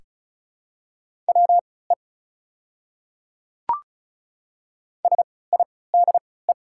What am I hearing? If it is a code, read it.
Morse code, heard as a keyed tone, about 34 wpm, spaced as for 20 wpm, 690 Hz. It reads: WE SIDE
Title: 3x Character Spacing - Top 100 Words - 35wpm